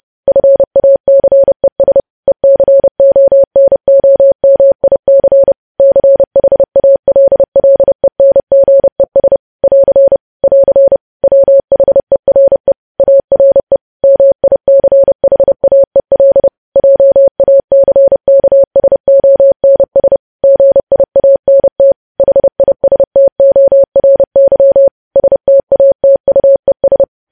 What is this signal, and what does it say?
Morse code, heard as a keyed tone, about 30 wpm, 550 Hz. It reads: FACES ECONOMIC CHALLENGES <AR> <AR> WHERE ARE MICHAEL JACKSONS GIANT HISTORY STATUES
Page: news headlines in morse code at 30 wpm